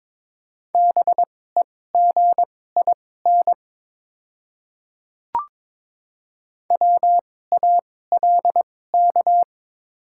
Morse code, heard as a keyed tone, about 22 wpm, spaced as for 16 wpm, 690 Hz. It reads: BEGIN WALK
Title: 2x Character Spacing - Top 300 Words - 22wpm